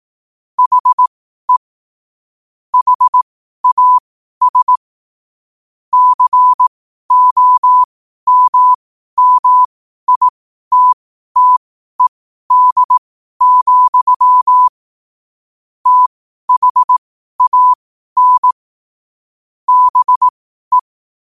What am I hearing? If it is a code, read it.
Morse code, heard as a keyed tone, about 18 wpm, spaced as for 12 wpm, 1000 Hz. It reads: HE HAS COMMITTED, THAN BE
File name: qotdfiles / 12wpm